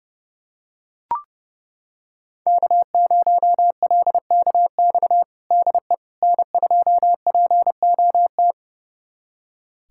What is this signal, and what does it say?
Morse code, heard as a keyed tone, about 30 wpm, 700 Hz. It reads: K0LKX DE N2POT
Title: USA Callsign DE Callsign - 5 Characters - 30wpm